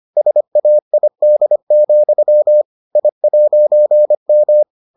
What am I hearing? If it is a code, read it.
Morse code, heard as a keyed tone, about 25 wpm, 600 Hz. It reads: SAID, I'M